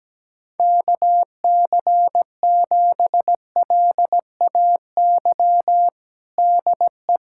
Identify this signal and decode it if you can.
Morse code, heard as a keyed tone, about 17 wpm, 700 Hz. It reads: KC7LAY DE
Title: USA Callsign DE Callsign - 6 Characters - 17wpm